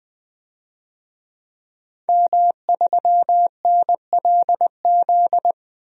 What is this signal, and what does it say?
Morse code, heard as a keyed tone, about 21 wpm, 700 Hz. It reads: M3NLZ
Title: DXCC-100 Call Signs - 5 Characters - 20wpm